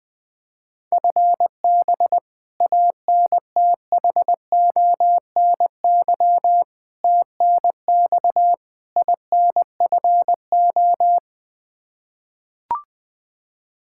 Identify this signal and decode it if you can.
Morse code, heard as a keyed tone, about 20 wpm, 700 Hz. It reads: FB ANTHONY TNX INFO
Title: Practice QSO - Exchange 3 - More Info - 20wpm